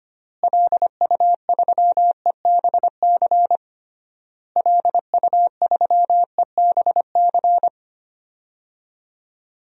Morse code, heard as a keyed tone, about 25 wpm, 700 Hz. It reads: LU3EBC LU3EBC
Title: DXCC-100 Call Signs - 6 Characters - Repeated 3x - 25wpm